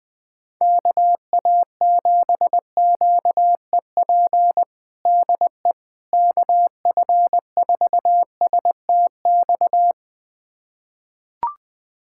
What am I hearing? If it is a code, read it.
Morse code, heard as a keyed tone, about 20 wpm, 700 Hz. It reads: KA7QEP DE KF4STX
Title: USA Callsign DE Callsign 20wpm